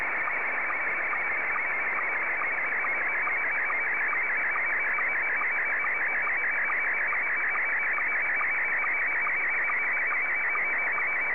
SSTV (slow-scan television), Scottie 1.